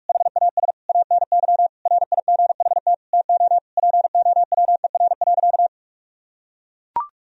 Morse code, heard as a keyed tone, about 41 wpm, 700 Hz. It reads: HAS ANY RIGHT TO POWER.
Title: Sentences from Top 1000 Words - Review Part 2 - 45wpm